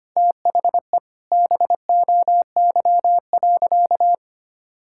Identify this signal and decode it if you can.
Morse code, heard as a keyed tone, about 25 wpm, 700 Hz. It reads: THE BOY.
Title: Sentences from Top 1000 Words - Review Part 6 - 25wpm